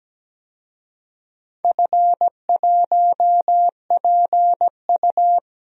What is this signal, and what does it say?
Morse code, heard as a keyed tone, about 17 wpm, 700 Hz. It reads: F1PU
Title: DXCC-100 Call Signs - 4 Characters - 17wpm